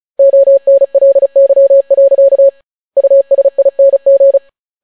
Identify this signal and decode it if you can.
Morse code, heard as a keyed tone, about 36 wpm, 550 Hz. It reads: ONLY. USING